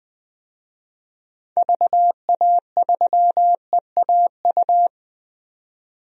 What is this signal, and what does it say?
Morse code, heard as a keyed tone, about 20 wpm, 700 Hz. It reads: VA3EAU
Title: DXCC-100 Call Signs - 6 Characters - 20wpm